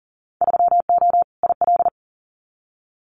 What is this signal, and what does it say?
Morse code, heard as a keyed tone, about 38 wpm, 700 Hz. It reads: 3O IL
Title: Winter Field Day Exchanges 40wpm